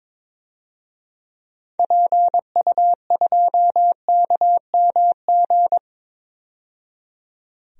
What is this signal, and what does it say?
Morse code, heard as a keyed tone, about 22 wpm, 700 Hz. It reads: PU2KMG
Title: DXCC-100 Call Signs - 6 Characters - 22wpm